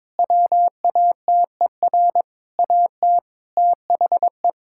Morse code, heard as a keyed tone, about 22 wpm, 700 Hz. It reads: WATER AT THE
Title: Sentences from Top 1000 Words - Review Part 6 - 22wpm